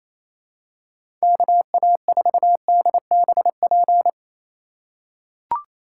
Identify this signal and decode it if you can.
Morse code, heard as a keyed tone, about 28 wpm, 700 Hz. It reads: KA4DBP
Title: USA Call Signs - 6 Characters - 2x Character Spacing - 28wpm